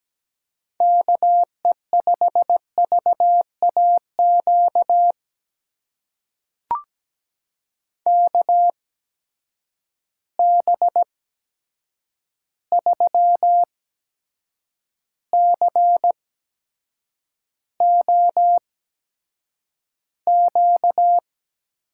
Morse code, heard as a keyed tone, about 17 wpm, spaced as for 7 wpm, 700 Hz. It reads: KE5VAQ KB3COQ
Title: USA Call Signs - 6 Characters - 8x Character Spacing - 17wpm